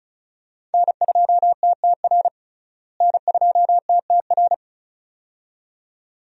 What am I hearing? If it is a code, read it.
Morse code, heard as a keyed tone, about 36 wpm, 700 Hz. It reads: N2TTR N2TTR
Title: USA Call Signs - 5 Characters - Repeated 2x - 35wpm